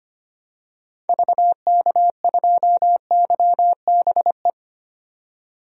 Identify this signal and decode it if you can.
Morse code, heard as a keyed tone, about 25 wpm, 700 Hz. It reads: VK2YBE
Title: DXCC-100 Call Signs - 6 Characters - 25wpm